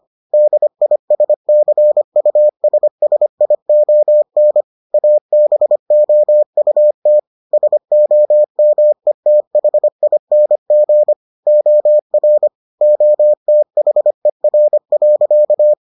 Morse code, heard as a keyed tone, about 25 wpm, 600 Hz. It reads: DISCUSSION ABOUT SOMETHING OR OTHER.